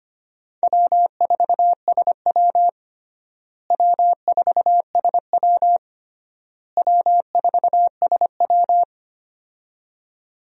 Morse code, heard as a keyed tone, about 25 wpm, 700 Hz. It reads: W4SW W4SW W4SW
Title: USA Call Signs - 4 Characters - Repeated 3x - 25wpm